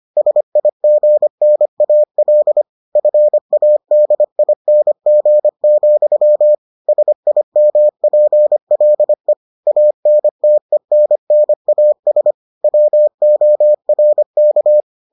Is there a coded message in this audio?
Yes: Morse code from 0.2 to 14.8 s, heard as a keyed tone, about 25 wpm, 600 Hz. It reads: SIGNAL FADING, SIMPLE ANTENNAS WORK